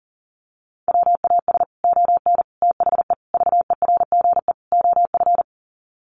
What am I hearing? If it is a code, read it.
Morse code, heard as a keyed tone, about 40 wpm, 700 Hz. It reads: WAS ON THE VERGE OF
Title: Sets of 5 Words Encore 40wpm